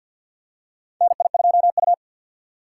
Morse code, heard as a keyed tone, about 52 wpm, 700 Hz. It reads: NI2U
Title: USA Call Signs - 4 Characters - 2x Character Spacing - 50wpm